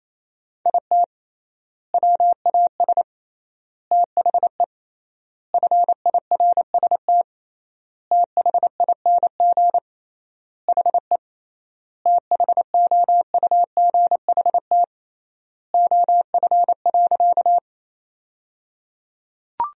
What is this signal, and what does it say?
Morse code, heard as a keyed tone, about 28 wpm, 700 Hz. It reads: IT WAS THE FIRST THING HE THOUGHT OF.
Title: 3x Word-Spacing - Sentences from Top 300 Words - 28wpm